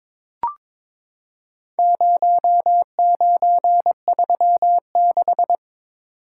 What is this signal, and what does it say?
Morse code, heard as a keyed tone, about 22 wpm, 700 Hz. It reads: 0936
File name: practice / 22wpm / Four Letter-Number Combinations - 22wpm